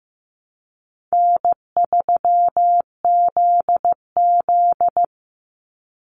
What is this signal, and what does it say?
Morse code, heard as a keyed tone, about 15 wpm, 700 Hz. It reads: N3ZZ
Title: Easy Call Signs Repeated 2x - 15wpm